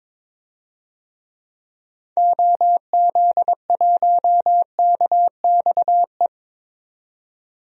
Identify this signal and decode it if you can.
Morse code, heard as a keyed tone, about 22 wpm, 700 Hz. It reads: OZ1KXE
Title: DXCC-100 Call Signs - 6 Characters - 22wpm